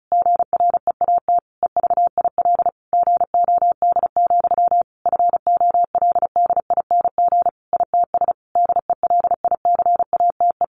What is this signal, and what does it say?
Morse code, heard as a keyed tone, about 34 wpm, 700 Hz. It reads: GREAT EVIL GOD, FOLDING ITS DELICATE
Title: A Sound Of Thunder 35wpm